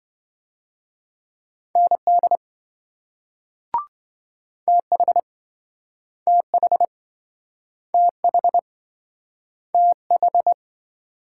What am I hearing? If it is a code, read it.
Morse code, heard as a keyed tone, about 28 wpm, 700 Hz. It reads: ND TH TH TH TH